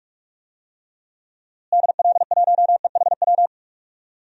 Morse code, heard as a keyed tone, about 44 wpm, 700 Hz. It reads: DL1EHW
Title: DXCC-100 Call Signs - 6 Characters - Repeated 2x - 45wpm